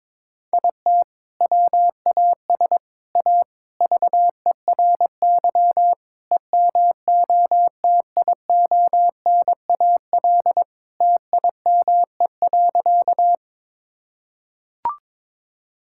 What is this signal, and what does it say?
Morse code, heard as a keyed tone, about 22 wpm, 700 Hz. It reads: IT WAS A VERY EMOTIONAL TIME.
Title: Sentences from Top 1000 Words - Review Part 5 - 22wpm